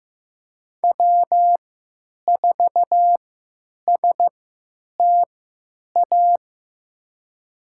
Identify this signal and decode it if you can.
Morse code, heard as a keyed tone, about 15 wpm, spaced as for 9 wpm, 700 Hz. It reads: W4STA
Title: USA Call Signs - 3x Character Spacing - 15wpm